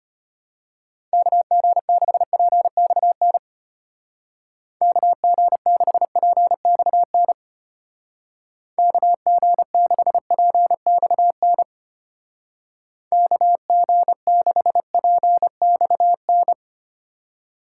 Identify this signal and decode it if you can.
Morse code, heard as a keyed tone, about 30 wpm, 700 Hz. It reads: KG6PXN KG6PXN KG6PXN KG6PXN
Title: Speed Racing - USA 2x3 Call Signs 25wpm